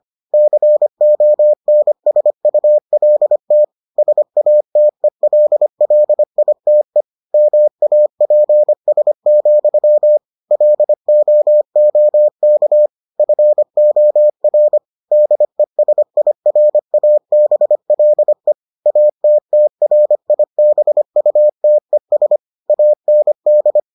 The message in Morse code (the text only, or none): CONSULT SATELLITE MAPS, LOOK FOR DESIRABLE ATTRIBUTES AND